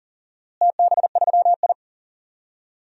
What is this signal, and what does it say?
Morse code, heard as a keyed tone, about 40 wpm, 700 Hz. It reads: TB3I
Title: DXCC-100 Call Signs - 4 Characters - Repeated 3x - 40wpm